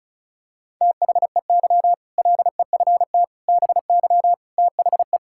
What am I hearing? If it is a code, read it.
Morse code, heard as a keyed tone, about 34 wpm, 700 Hz. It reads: THEY LEFT BY THE
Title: Sentences from Top 1600 Words 35wpm